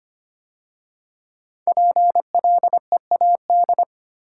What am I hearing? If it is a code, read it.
Morse code, heard as a keyed tone, about 25 wpm, 700 Hz. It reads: PLEAD
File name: 5 Letter Words - First 3 Letters Given - 25wpm